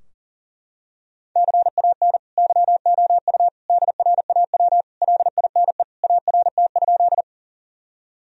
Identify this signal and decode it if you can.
Morse code, heard as a keyed tone, about 40 wpm, 700 Hz. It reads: CAN YOU DRAW LINE ART?
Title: Sentences from Top 1000 Words - Review Part 3 - 40wpm